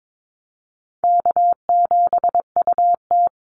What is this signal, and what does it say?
Morse code, heard as a keyed tone, about 22 wpm, 700 Hz. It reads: K7UT
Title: Easy Call Signs Repeated 2x - 22wpm